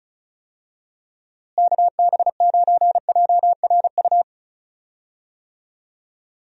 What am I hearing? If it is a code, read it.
Morse code, heard as a keyed tone, about 36 wpm, 700 Hz. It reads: KB9JRU
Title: USA Call Signs - 6 Characters - Repeated 2x - 35wpm